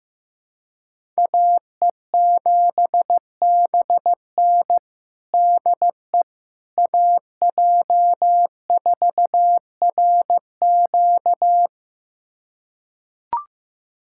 Morse code, heard as a keyed tone, about 15 wpm, 700 Hz. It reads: AE7BN DE AJ4RQ
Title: USA Callsign DE Callsign - 5 Characters - 15wpm